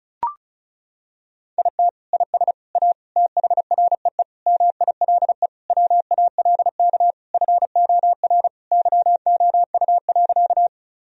Morse code, heard as a keyed tone, about 36 wpm, 700 Hz. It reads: IT IS A THREE MILE WALK FOR YOU.
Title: Sentences from Top 1000 Words - Review Part 1 - 35wpm